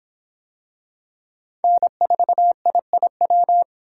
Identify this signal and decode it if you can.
Morse code, heard as a keyed tone, about 26 wpm, 700 Hz. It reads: N4IIW